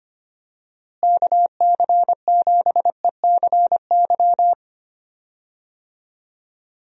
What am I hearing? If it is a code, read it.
Morse code, heard as a keyed tone, about 25 wpm, 700 Hz. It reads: KC7ECY
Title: USA Call Signs - 6 Characters - Repeated 3x - 25wpm